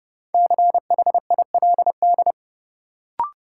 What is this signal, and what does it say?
Morse code, heard as a keyed tone, about 30 wpm, 700 Hz. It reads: CHILD
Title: Sentences from Top 800 Words - New Word Drill - 30wpm